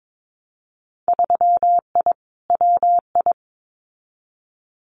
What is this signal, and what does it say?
Morse code, heard as a keyed tone, about 22 wpm, 700 Hz. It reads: 3I WI
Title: Winter Field Day Exchanges 22wpm